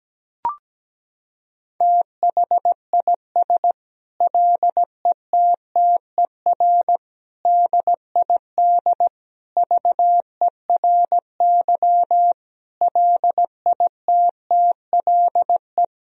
Morse code, heard as a keyed tone, about 17 wpm, 700 Hz. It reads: THIS LETTER DID VERY LITTLE
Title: Sentences from Top 1000 Words - Review Part 4 - 17wpm